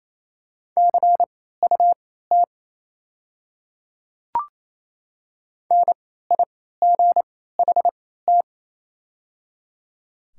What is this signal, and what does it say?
Morse code, heard as a keyed tone, about 28 wpm, spaced as for 16 wpm, 700 Hz. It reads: CUT NIGHT
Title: 3x Character Spacing - Top 500 Words - 28wpm